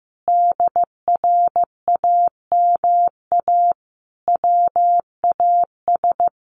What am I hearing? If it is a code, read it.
Morse code, heard as a keyed tone, about 15 wpm, 700 Hz. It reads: DRAMA WAS
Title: Sentences from Top 1500 Words 15wpm